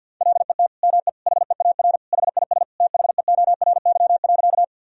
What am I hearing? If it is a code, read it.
Morse code, heard as a keyed tone, about 51 wpm, 700 Hz. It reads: LET ME HEAR HIS THEORY.
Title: Sentences from Top 1000 Words - Review Part 3 - 50wpm